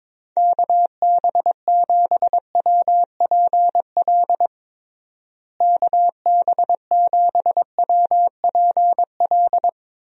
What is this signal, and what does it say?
Morse code, heard as a keyed tone, about 22 wpm, 700 Hz. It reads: KB7WPL KB7WPL
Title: USA Call Signs - 6 Characters - Repeated 2x - 22wpm